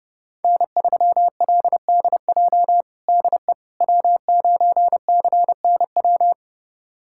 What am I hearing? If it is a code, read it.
Morse code, heard as a keyed tone, about 32 wpm, 700 Hz. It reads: N3LDJ DE W9CNW
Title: USA Callsign DE Callsign - 5 Characters - 30wpm